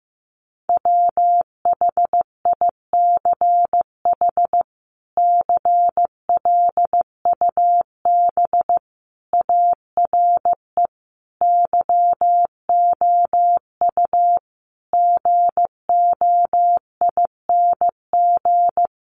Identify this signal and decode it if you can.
Morse code, heard as a keyed tone, about 15 wpm, 700 Hz. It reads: WHICH CLUB ARE YOU GOING
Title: Sentences from Top 1200 Words 15wpm